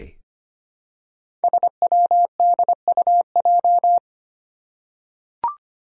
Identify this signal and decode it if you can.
Morse code, heard as a keyed tone, about 25 wpm, 700 Hz. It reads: SWDUJ
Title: Five Challenging Letter-Number Combinations 25wpm